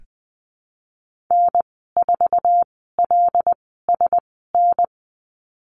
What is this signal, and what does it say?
Morse code, heard as a keyed tone, about 20 wpm, spaced as for 14 wpm, 700 Hz. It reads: N4LSN
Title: 088 Lesson 15g - Number 5 - Call Signs 20wpm